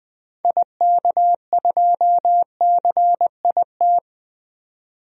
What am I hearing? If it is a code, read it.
Morse code, heard as a keyed tone, about 21 wpm, 700 Hz. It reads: IK2CIT
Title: DXCC-100 Call Signs - 6 Characters - 20wpm